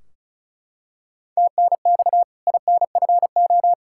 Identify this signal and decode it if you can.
Morse code, heard as a keyed tone, about 36 wpm, 700 Hz. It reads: TNX INFO
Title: K1USN SST QSO Practice 35wpm